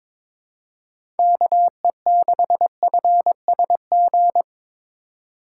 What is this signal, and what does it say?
Morse code, heard as a keyed tone, about 22 wpm, 700 Hz. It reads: KE6FSG
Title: USA Call Signs - 6 Characters - 3x Character Spacing - 22wpm